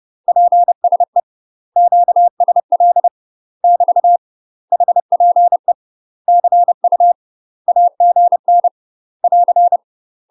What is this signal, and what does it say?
Morse code, heard as a keyed tone, about 30 wpm, 700 Hz. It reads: PSE QSL <BT> HPE CU AGN <AR>
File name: UK MASTER QSOs TRAINING / 30WPM / UK MASTER QSOs 0041